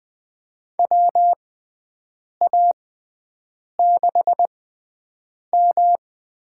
Morse code, heard as a keyed tone, about 20 wpm, spaced as for 7 wpm, 700 Hz. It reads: WA6M
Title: USA Call Signs - 6x Character Spacing - 20wpm